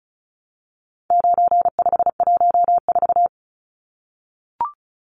Morse code, heard as a keyed tone, about 34 wpm, 700 Hz. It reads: 9514
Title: Numbers - 4-Digit 35wpm